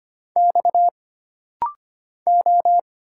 Morse code, heard as a keyed tone, about 25 wpm, 700 Hz. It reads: X O
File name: Single Character (Letter:Number:Punctuation:Prosign) - Rapid-Fire - 25wpm